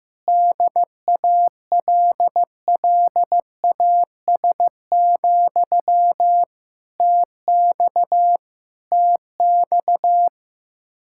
Morse code, heard as a keyed tone, about 16 wpm, 700 Hz. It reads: DALLAS, TX TX
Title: Practice QSO - Exchange 1 - Beginning - 15wpm